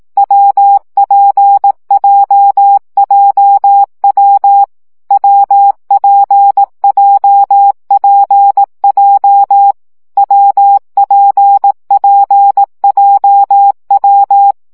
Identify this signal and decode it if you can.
Morse code, heard as a keyed tone, about 18 wpm, 800 Hz. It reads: WPJJW WPJPJ WPPJW
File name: CW gyakorlas w p j 15 baud